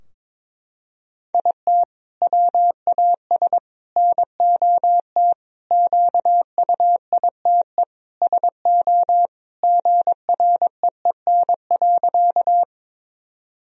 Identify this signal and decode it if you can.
Morse code, heard as a keyed tone, about 22 wpm, 700 Hz. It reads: IT WAS NOT QUITE SO GREEN.
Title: Sentences from Top 1000 Words - Review Part 2 - 22wpm